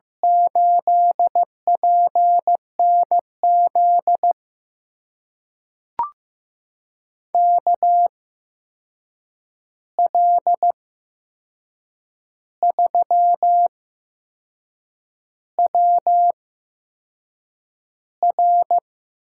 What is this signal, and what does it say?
Morse code, heard as a keyed tone, about 15 wpm, spaced as for 6 wpm, 700 Hz. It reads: 8PNZ KL3WR
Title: USA Call Signs - 5 Characters - 8x Character Spacing - 15wpm